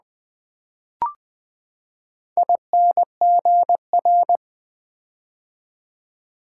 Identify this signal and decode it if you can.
Morse code, heard as a keyed tone, about 20 wpm, 700 Hz. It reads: INGR
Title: Build A Word - 4 Letters At A Time - 10-Letter Words - 20wpm